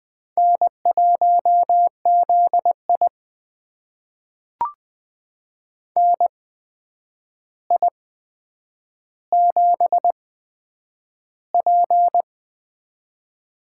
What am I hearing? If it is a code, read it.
Morse code, heard as a keyed tone, about 20 wpm, spaced as for 9 wpm, 700 Hz. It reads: N1ZI NI7P
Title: USA Call Signs - 4 Characters - 8x Character Spacing - 20wpm